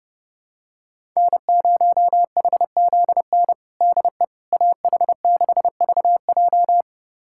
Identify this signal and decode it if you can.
Morse code, heard as a keyed tone, about 30 wpm, 700 Hz. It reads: N0HZN DE AH6VJ
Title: USA Callsign DE Callsign - 5 Characters - 30wpm